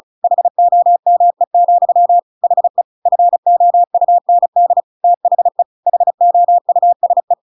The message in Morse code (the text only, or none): HOME, HE FOUND THE HOUSE